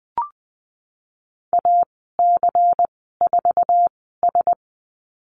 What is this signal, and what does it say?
Morse code, heard as a keyed tone, about 20 wpm, spaced as for 15 wpm, 700 Hz. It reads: AC4S
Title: 147 Lesson 24g - Letter y - Call Signs 20wpm